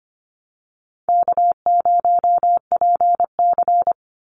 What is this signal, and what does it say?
Morse code, heard as a keyed tone, about 25 wpm, 700 Hz. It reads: K0PC
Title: Easy Call Signs - 25wpm